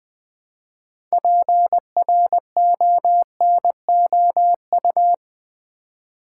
Morse code, heard as a keyed tone, about 20 wpm, 700 Hz. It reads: PRONOU
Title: Build A Word - 3 Letters At A Time - 9-Letter Words - 20wpm